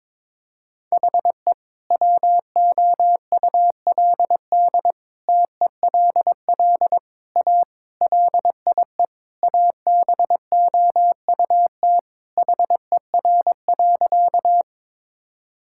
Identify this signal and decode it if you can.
Morse code, heard as a keyed tone, about 22 wpm, 700 Hz. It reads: HE WOULD TELL A LIE ABOUT HER.
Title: Sentences from Top 1600 Words 22wpm